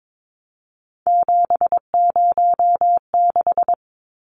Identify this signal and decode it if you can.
Morse code, heard as a keyed tone, about 22 wpm, 700 Hz. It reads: 706
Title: Numbers - 3-Digit 22wpm